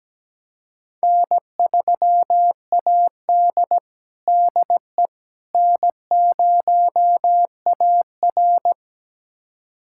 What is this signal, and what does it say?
Morse code, heard as a keyed tone, about 17 wpm, 700 Hz. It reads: N3AD DE N0AR
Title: USA Callsign DE Callsign - 4 Characters - 17wpm